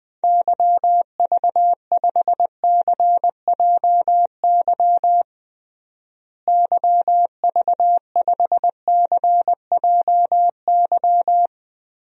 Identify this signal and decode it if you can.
Morse code, heard as a keyed tone, about 20 wpm, 700 Hz. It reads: YV5CJY YV5CJY
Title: DXCC-100 Call Signs - 6 Characters - Repeated 2x - 20wpm